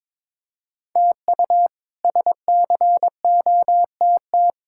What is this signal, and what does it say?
Morse code, heard as a keyed tone, about 22 wpm, 700 Hz. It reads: TU SCOTT
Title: K1USN SST QSO Practice - 22wpm